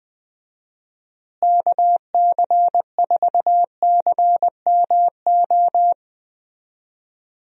Morse code, heard as a keyed tone, about 20 wpm, 700 Hz. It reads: KC4CMO
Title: USA Call Signs - 6 Characters - Repeated 3x - 20wpm